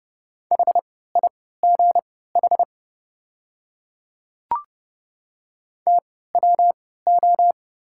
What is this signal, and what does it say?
Morse code, heard as a keyed tone, about 32 wpm, spaced as for 17 wpm, 690 Hz. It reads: HIGH TWO
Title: 3x Character Spacing - Top 300 Words - 30wpm